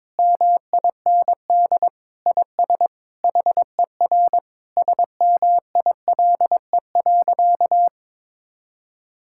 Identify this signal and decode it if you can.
Morse code, heard as a keyed tone, about 22 wpm, 700 Hz. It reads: MIND IS HER SMILE.
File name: practice / 22wpm / Sentences with 8-Word Phrases - 22wpm